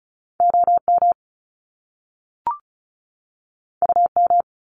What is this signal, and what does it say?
Morse code, heard as a keyed tone, about 36 wpm, 700 Hz. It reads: OM UM
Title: Top Two Letter Combinations in Words 35wpm